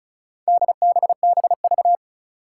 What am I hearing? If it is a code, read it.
Morse code, heard as a keyed tone, about 36 wpm, 700 Hz. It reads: DBBV
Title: Four Challenging Letter-Number Combinations 35wpm